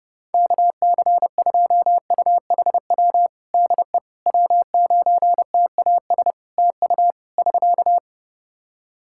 Morse code, heard as a keyed tone, about 32 wpm, 700 Hz. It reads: KC2UHW DE W9TAS TU <SK>
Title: Practice QSO - Exchange 4 - Ending - 30wpm